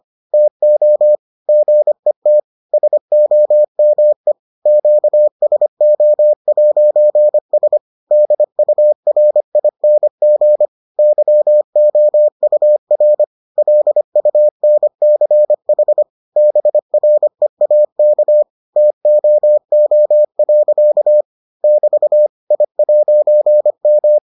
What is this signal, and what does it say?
Morse code, heard as a keyed tone, about 25 wpm, 600 Hz. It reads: TO GET SOME QSO'S DURING YOUR LUNCH BREAK TOO. <BT> I'M